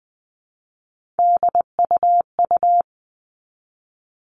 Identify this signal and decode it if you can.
Morse code, heard as a keyed tone, about 20 wpm, 700 Hz. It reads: DUU
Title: All Cut Numbers - 3-Digit 20wpm